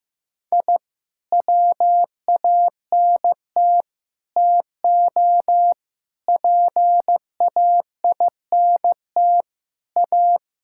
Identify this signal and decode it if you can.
Morse code, heard as a keyed tone, about 15 wpm, 700 Hz. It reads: I WANT TO PAINT A
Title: Sentences from Top 1000 Words - Review Part 6 - 15wpm